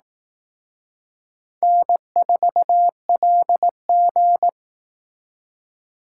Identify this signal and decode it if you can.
Morse code, heard as a keyed tone, about 18 wpm, 700 Hz. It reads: N4LG
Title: Speed Racing - USA 1x2 Call Signs 15wpm